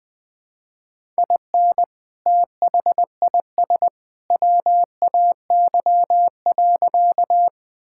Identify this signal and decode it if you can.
Morse code, heard as a keyed tone, about 20 wpm, 700 Hz. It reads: IN THIS WAY.
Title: Sentences with 2-Word Phrases - 20wpm